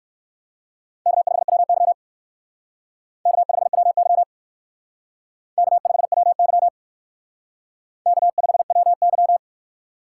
Morse code, heard as a keyed tone, about 59 wpm, 700 Hz. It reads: K5WY K5WY K5WY K5WY